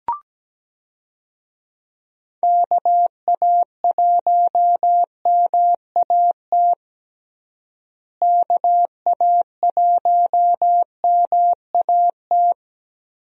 Morse code, heard as a keyed tone, about 17 wpm, 700 Hz. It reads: KA1MAT KA1MAT
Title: USA Call Signs - 6 Characters - Repeated 3x - 17wpm